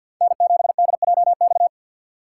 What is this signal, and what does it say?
Morse code, heard as a keyed tone, about 49 wpm, 700 Hz. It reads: N7DJX
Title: USA Call Signs - 5 Characters - 3x Character Spacing - 50wpm